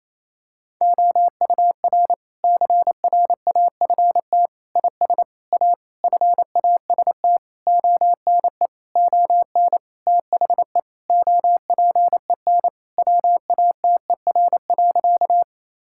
Morse code, heard as a keyed tone, about 28 wpm, 700 Hz. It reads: OUR CRAFT IS A FAST ONE ON THE OPEN WATER.
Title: Sentences from Top 1000 Words - Review Part 6 - 28wpm